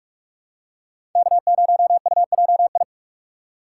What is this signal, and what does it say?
Morse code, heard as a keyed tone, about 46 wpm, 700 Hz. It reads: K0UJI
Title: USA Call Signs - 5 Characters - 2x Character Spacing - 45wpm